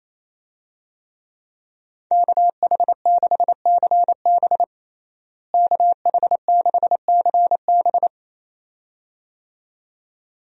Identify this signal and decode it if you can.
Morse code, heard as a keyed tone, about 28 wpm, 700 Hz. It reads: KH6CB KH6CB
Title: USA Call Signs - 5 Characters - Repeated 2x - 28wpm